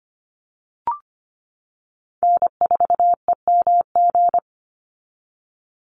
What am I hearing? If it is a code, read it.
Morse code, heard as a keyed tone, about 25 wpm, 700 Hz. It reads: N4EMG
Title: NA SOTA Chaser Callsigns 25wpm